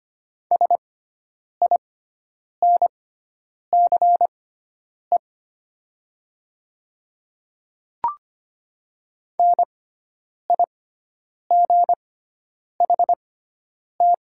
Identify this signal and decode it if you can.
Morse code, heard as a keyed tone, about 25 wpm, spaced as for 9 wpm, 700 Hz. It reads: SINCE NIGHT
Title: 6x Character Spacing - Top 300 Words - 25wpm